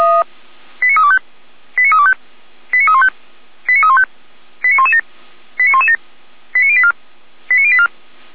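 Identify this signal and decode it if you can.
Five-tone selective call (ZVEI), heard as quick runs of five tones, tones of about 70 ms. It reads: 89325 89325 89315 89315 89117 89117 89084 89084
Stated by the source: Five-tone ZVEI selective call 89104 received by SDR, logged by fire brigade Feuerwehr Orlamünde.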